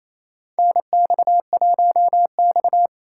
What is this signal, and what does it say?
Morse code, heard as a keyed tone, about 28 wpm, 700 Hz. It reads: NX1X
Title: USA Call Signs - 4x Character Spacing - 28wpm